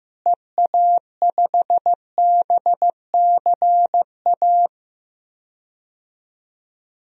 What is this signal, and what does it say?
Morse code, heard as a keyed tone, about 15 wpm, 700 Hz. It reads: EA5BCA
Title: DXCC-100 Call Signs - 6 Characters - Repeated 2x - 15wpm